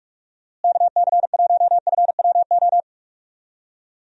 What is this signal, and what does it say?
Morse code, heard as a keyed tone, about 44 wpm, 700 Hz. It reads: KC1FWO
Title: USA Call Signs - 6 Characters - Repeated 3x - 45wpm